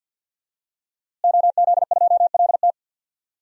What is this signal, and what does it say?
Morse code, heard as a keyed tone, about 48 wpm, 700 Hz. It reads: OZ2LT
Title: DXCC-100 Call Signs - 5 Characters - Repeated 2x - 50wpm